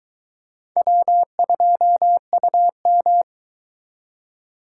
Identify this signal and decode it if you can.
Morse code, heard as a keyed tone, about 23 wpm, 700 Hz. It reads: W2UM